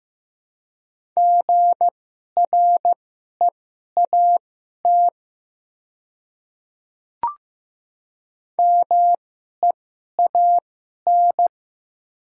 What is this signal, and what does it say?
Morse code, heard as a keyed tone, about 15 wpm, spaced as for 11 wpm, 700 Hz. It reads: GREAT MEAN
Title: 2x Character Spacing - Top 300 Words - 15wpm